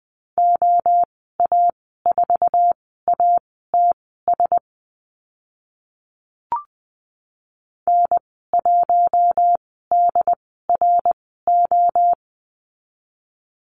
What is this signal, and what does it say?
Morse code, heard as a keyed tone, about 20 wpm, spaced as for 14 wpm, 700 Hz. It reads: OA4ATS N1DRO
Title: 076 Lesson 13g - Letter L - Call Signs 20wpm